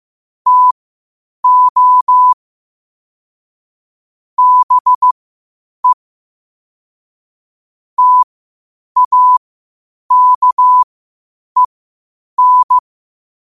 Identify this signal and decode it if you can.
Morse code, heard as a keyed tone, about 15 wpm, spaced as for 8 wpm, 1000 Hz. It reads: TO BE TAKEN
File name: qotdfiles / 8wpm